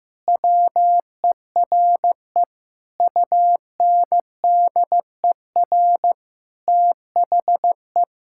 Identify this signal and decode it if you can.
Morse code, heard as a keyed tone, about 14 wpm, 700 Hz. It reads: WERE UNDER THE
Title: Sentences from Top 1600 Words 15wpm